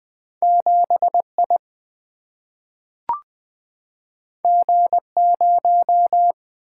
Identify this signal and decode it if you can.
Morse code, heard as a keyed tone, about 20 wpm, 700 Hz. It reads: 7I G0
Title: Single-Letter Single-Number 20wpm